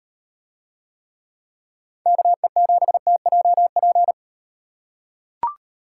Morse code, heard as a keyed tone, about 38 wpm, 700 Hz. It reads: KE7TJP